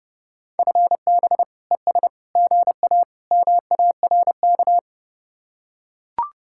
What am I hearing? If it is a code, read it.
Morse code, heard as a keyed tone, about 32 wpm, 700 Hz. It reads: FB ES GA MARK